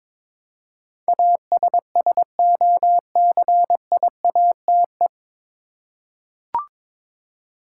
Morse code, heard as a keyed tone, about 21 wpm, 700 Hz. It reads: ASSOCIATE
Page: Morse Code Ninja practice 22wpm